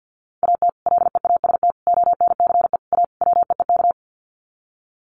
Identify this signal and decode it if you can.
Morse code, heard as a keyed tone, about 50 wpm, 700 Hz. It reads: AT LEAST ONCE A WEEK
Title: Sets of 5 Words Encore2 50wpm